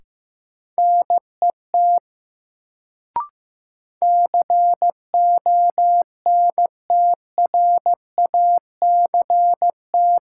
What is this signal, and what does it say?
Morse code, heard as a keyed tone, about 15 wpm, 700 Hz. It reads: NET CONTRACT
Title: Sentences from Top 1400 Words - New Word Drill - 15wpm